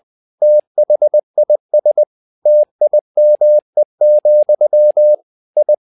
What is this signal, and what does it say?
Morse code, heard as a keyed tone, about 20 wpm, 600 Hz. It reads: THIS TIME, I
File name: tower trouble 20wpm